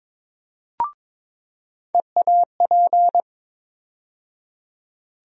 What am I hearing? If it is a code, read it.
Morse code, heard as a keyed tone, about 21 wpm, 700 Hz. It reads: EAP
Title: USA Call Sign Postfixes - 3 Letters 22wpm